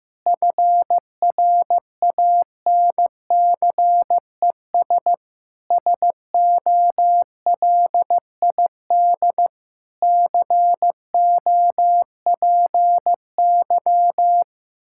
Morse code, heard as a keyed tone, about 15 wpm, 700 Hz. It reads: FRANCES SOLID COPY
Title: Practice QSO - Exchange 3 - More Info - 15wpm